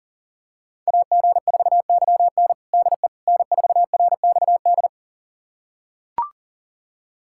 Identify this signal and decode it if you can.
Morse code, heard as a keyed tone, about 40 wpm, 700 Hz. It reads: AG4YN DE N4RXD
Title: USA Callsign DE Callsign - 5 Characters - 40wpm